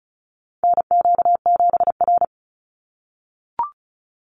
Morse code, heard as a keyed tone, about 34 wpm, 700 Hz. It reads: NQ7R
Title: NA SOTA Chaser Callsigns 35wpm